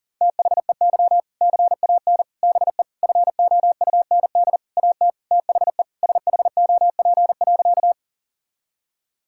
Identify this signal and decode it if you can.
Morse code, heard as a keyed tone, about 37 wpm, 700 Hz. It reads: THEY CAN BE FOUND AT THE SHOP.
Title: Sentences from Top 1600 Words 40wpm